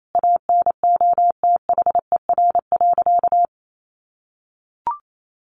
Morse code, heard as a keyed tone, about 28 wpm, 700 Hz. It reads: ANOTHER.
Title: Sentences from Top 1400 Words 28wpm